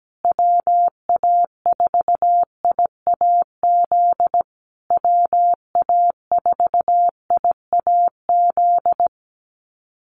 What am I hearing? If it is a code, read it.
Morse code, heard as a keyed tone, about 17 wpm, 700 Hz. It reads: WA4IAZ WA4IAZ
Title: USA Call Signs Repeated 3x 17wpm